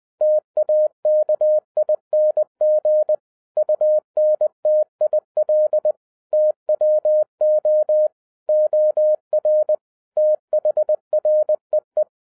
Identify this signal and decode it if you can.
Morse code, heard as a keyed tone, about 20 wpm, 610 Hz. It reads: TAKING UNTIL TWO OR THREE
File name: 2011-07-mp3 / n33 THE 3.11 2 20wpm